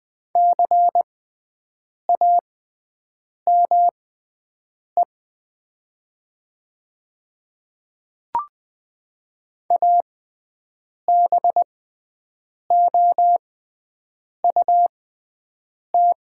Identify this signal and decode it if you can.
Morse code, heard as a keyed tone, about 20 wpm, spaced as for 7 wpm, 700 Hz. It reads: CAME ABOUT
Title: 6x Character Spacing - Top 200 Words - 20wpm